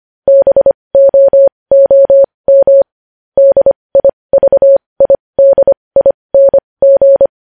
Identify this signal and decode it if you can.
Morse code, heard as a keyed tone, about 25 wpm, spaced as for 20 wpm, 550 Hz. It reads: BOOM DIVIDING
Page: news-headlines-in-morse-code-at-20-wpm